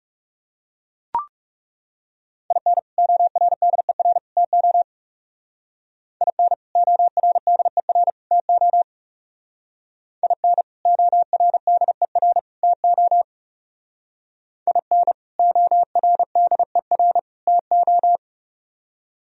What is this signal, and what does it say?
Morse code, heard as a keyed tone, about 36 wpm, 700 Hz. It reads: IN ORDER TO IN ORDER TO IN ORDER TO IN ORDER TO
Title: Speed-Racing - Sets of 3 Words 30wpm